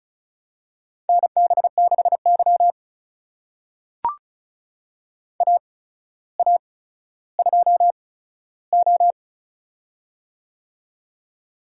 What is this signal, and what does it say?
Morse code, heard as a keyed tone, about 34 wpm, spaced as for 15 wpm, 700 Hz. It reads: NB6Y AA2O
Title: USA Call Signs - 4 Characters - 8x Character Spacing - 35wpm